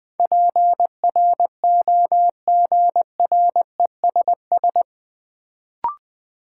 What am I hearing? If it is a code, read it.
Morse code, heard as a keyed tone, about 21 wpm, 700 Hz. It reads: PROGRESS
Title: Sentences from Top 1400 Words - New Word Drill Alt - 20wpm